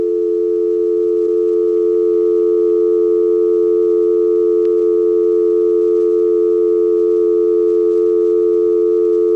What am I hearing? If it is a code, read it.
Dial tone (North America), 350 + 440 Hz.
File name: openvoip